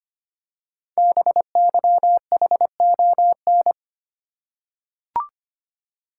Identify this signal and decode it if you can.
Morse code, heard as a keyed tone, about 25 wpm, 700 Hz. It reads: BYHON
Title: Code Groups - 5 Letters - 25wpm